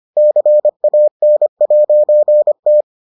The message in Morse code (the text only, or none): CAN'T